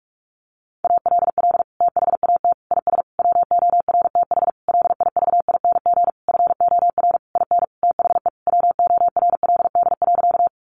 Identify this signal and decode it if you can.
Morse code, heard as a keyed tone, about 44 wpm, 700 Hz. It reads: ALL THAT IS WORTH LIVING FOR IN THE WORLD.
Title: Sentences from Top 1100 Words 45wpm